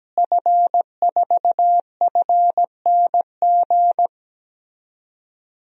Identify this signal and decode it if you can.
Morse code, heard as a keyed tone, about 17 wpm, 700 Hz. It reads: F4FNG
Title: DXCC-100 Call Signs - 5 Characters - Repeated 2x - 17wpm